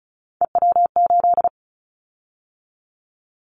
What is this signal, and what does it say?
Morse code, heard as a keyed tone, about 36 wpm, 700 Hz. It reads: EW8
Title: Callsign Prefixes for Top 50 DXCC - 35wpm